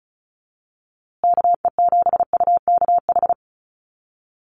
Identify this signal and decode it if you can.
Morse code, heard as a keyed tone, about 34 wpm, 700 Hz. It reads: KE7UKH
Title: USA Call Signs Repeated 3x 35wpm